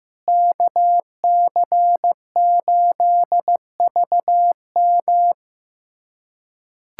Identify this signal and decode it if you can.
Morse code, heard as a keyed tone, about 16 wpm, 700 Hz. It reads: KC8VM
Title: USA Callsign DE Callsign - 5 Characters - 15wpm